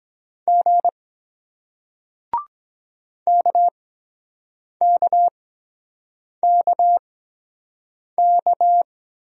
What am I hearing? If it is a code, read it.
Morse code, heard as a keyed tone, about 23 wpm, spaced as for 7 wpm, 700 Hz. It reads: G KKKK